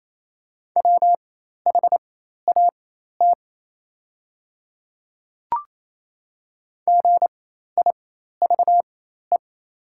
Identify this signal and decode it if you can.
Morse code, heard as a keyed tone, about 28 wpm, spaced as for 13 wpm, 700 Hz. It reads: WHAT GIVE
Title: 4x Character Spacing - Top 300 Words - 28wpm